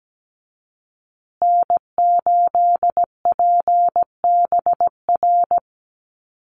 Morse code, heard as a keyed tone, about 17 wpm, 700 Hz. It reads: N8PBR
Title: USA Call Signs Repeated 2x 17wpm